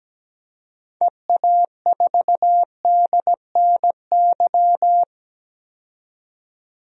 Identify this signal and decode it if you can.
Morse code, heard as a keyed tone, about 17 wpm, 700 Hz. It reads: EA4DNY
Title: DXCC-100 Call Signs - 6 Characters - 17wpm